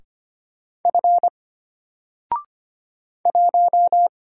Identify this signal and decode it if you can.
Morse code, heard as a keyed tone, about 25 wpm, 700 Hz. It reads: F 1
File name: Single-Letter-Number-25wpm